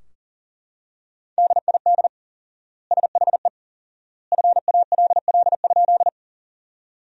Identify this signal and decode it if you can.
Morse code, heard as a keyed tone, about 40 wpm, 700 Hz. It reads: DID SHE FALL?
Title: 4x Word-Spacing - Sentences from Top 500 Words - 40wpm